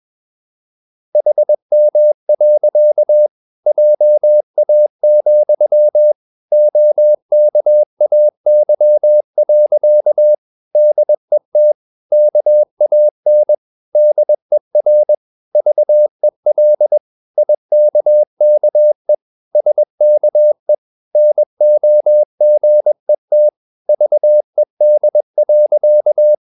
Morse code, heard as a keyed tone, about 21 wpm, 600 Hz. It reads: HM. JA, OKAY. DET KAN DER VEL IKKE SKE NOGET VED.
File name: Ferie QRS 21wpm